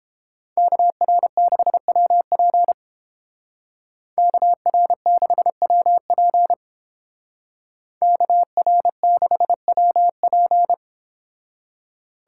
Morse code, heard as a keyed tone, about 29 wpm, 700 Hz. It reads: KR6WP KR6WP KR6WP